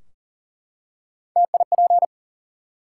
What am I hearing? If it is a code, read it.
Morse code, heard as a keyed tone, about 42 wpm, 700 Hz. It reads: TIP
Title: Three Letter Words - 40wpm